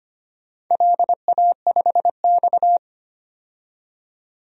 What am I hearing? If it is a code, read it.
Morse code, heard as a keyed tone, about 25 wpm, 700 Hz. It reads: LA5X
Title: DXCC-100 Call Signs - 4 Characters - 25wpm